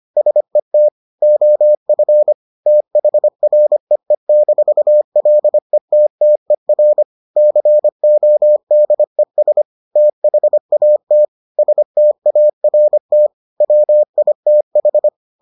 Morse code, heard as a keyed tone, about 25 wpm, 600 Hz. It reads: SET OF THREE-LETTER CODES THAT START WITH